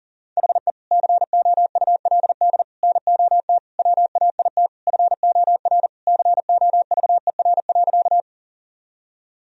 Morse code, heard as a keyed tone, about 37 wpm, 700 Hz. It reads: HE COULD NOT WAIT FOR COVER.
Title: Sentences from Top 1000 Words - Review Part 1 - 40wpm